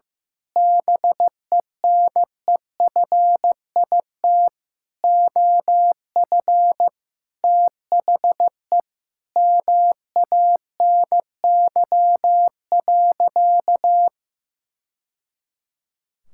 Morse code, heard as a keyed tone, about 15 wpm, 700 Hz. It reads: BENEFIT OF THE MANY.
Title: Sentences from Top 1000 Words - Review Part 6 - 15wpm